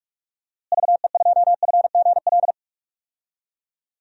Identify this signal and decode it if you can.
Morse code, heard as a keyed tone, about 41 wpm, 700 Hz. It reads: VE2FGL
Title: DXCC-100 Call Signs - 6 Characters - Repeated 3x - 45wpm